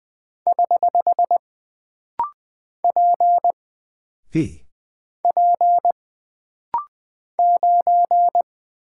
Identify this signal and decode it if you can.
Morse code, heard as a keyed tone, about 20 wpm, 700 Hz. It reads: <HH> P P 9
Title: Single Character (Letter:Number:Punctuation:Prosign) - Rapid-Fire - 20wpm